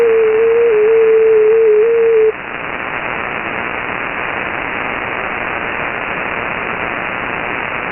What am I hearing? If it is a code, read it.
FT8.
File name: Radio / SDR Audio / gqrx